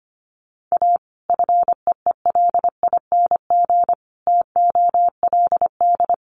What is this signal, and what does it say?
Morse code, heard as a keyed tone, about 25 wpm, 700 Hz. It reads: A FEELING TOLD
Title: Sentences from Top 1200 Words 25wpm